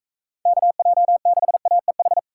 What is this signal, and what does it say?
Morse code, heard as a keyed tone, about 44 wpm, 700 Hz. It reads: KJ6AEH